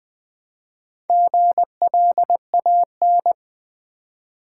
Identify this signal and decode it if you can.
Morse code, heard as a keyed tone, about 20 wpm, 700 Hz. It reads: GLAN
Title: Build A Word - 2 Letters At A Time - 6-Letter Words - 20wpm